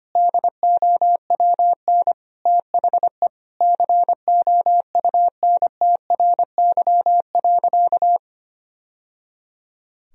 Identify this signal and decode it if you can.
Morse code, heard as a keyed tone, about 25 wpm, 700 Hz. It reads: DOWN THE COUNTRY.
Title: Sentences from Top 1000 Words - Review Part 2 - 25wpm